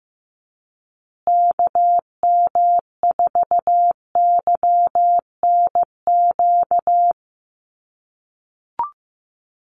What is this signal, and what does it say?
Morse code, heard as a keyed tone, about 15 wpm, 700 Hz. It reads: KM4YNQ
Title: USA Call Signs Repeated 2x 15wpm